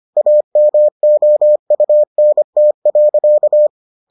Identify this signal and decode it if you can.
Morse code, heard as a keyed tone, about 25 wpm, 600 Hz. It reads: AMOUNT.